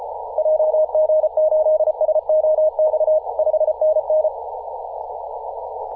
Morse code, heard as a keyed tone, about 34 wpm, 620 Hz. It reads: JG8SOX 5NN
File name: posts / 2017-aug-07-0652z-ri0li-14-036mhz-cw